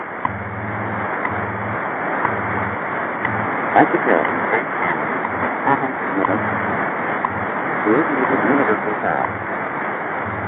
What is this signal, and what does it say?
Radio time signal, with speech.
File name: WWV 0830